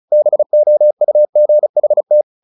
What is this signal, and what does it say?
Morse code, heard as a keyed tone, about 34 wpm, 600 Hz. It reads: BOUGHT
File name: the old man and the sea / the old man and the sea 1 35wpm